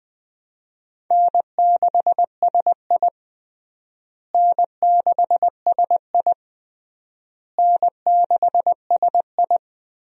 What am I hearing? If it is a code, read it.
Morse code, heard as a keyed tone, about 20 wpm, 700 Hz. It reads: N6SI N6SI N6SI
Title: USA Call Signs - 4 Characters - Repeated 3x - 20wpm